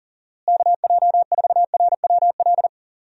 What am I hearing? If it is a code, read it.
Morse code, heard as a keyed tone, about 39 wpm, 700 Hz. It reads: KJ4RWL
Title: USA Call Signs - 6 Characters - Repeated 3x - 40wpm